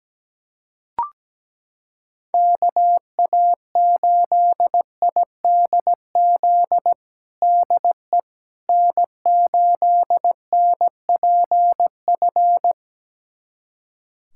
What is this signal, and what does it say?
Morse code, heard as a keyed tone, about 17 wpm, 700 Hz. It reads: KA8IDZ DE N8NPF
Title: USA Callsign DE Callsign 17wpm